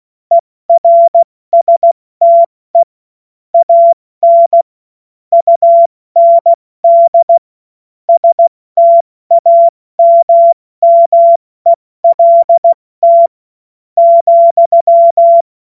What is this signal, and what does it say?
Morse code, heard as a keyed tone, about 16 wpm, spaced as for 13 wpm, 680 Hz. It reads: ERSTE AN UND STAMMELT,
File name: Morsekurs / Witze / Witz 09. November 2021, 70 F80